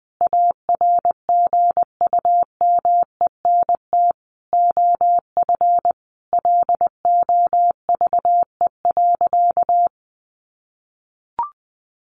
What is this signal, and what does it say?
Morse code, heard as a keyed tone, about 20 wpm, 700 Hz. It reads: ARGUMENT OF LOVE.
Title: Sentences from Top 1400 Words 20wpm